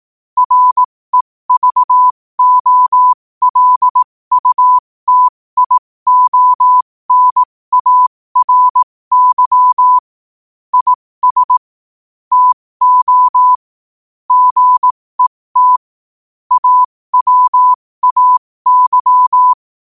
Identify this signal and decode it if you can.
Morse code, heard as a keyed tone, about 18 wpm, spaced as for 15 wpm, 1000 Hz. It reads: REVOLUTIONARY IS TO GET AWAY
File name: qotdfiles / 15wpm